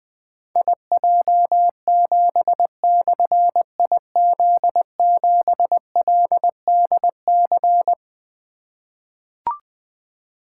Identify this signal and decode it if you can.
Morse code, heard as a keyed tone, about 20 wpm, 700 Hz. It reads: IJ7/IZ7LDC
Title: Difficult Call Signs 20wpm